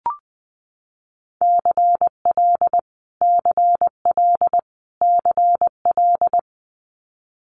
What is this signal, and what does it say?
Morse code, heard as a keyed tone, about 20 wpm, 700 Hz. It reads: CL CL CL
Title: Multi-letter Phonemes Repeated 3x 20wpm